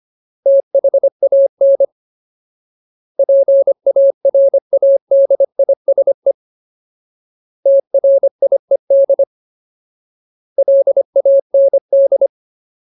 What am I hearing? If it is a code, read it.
Morse code, heard as a keyed tone, about 25 wpm, 550 Hz. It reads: THAN PARADISE TRIED LAND
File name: Ultimate Word List / 25WPM Ultimate